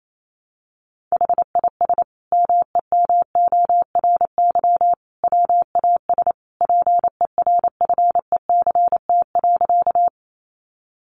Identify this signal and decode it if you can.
Morse code, heard as a keyed tone, about 28 wpm, 700 Hz. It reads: HIS MEMORY WAS PERFECT.
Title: Sentences from Top 1300 Words 28wpm